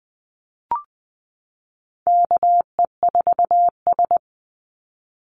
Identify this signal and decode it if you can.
Morse code, heard as a keyed tone, about 20 wpm, 700 Hz. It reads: KE4S
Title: Call Signs Repeated 3x - 20wpm